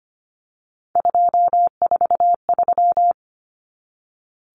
Morse code, heard as a keyed tone, about 25 wpm, 700 Hz. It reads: 243
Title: Common Cut Numbers - 3-Digit 25wpm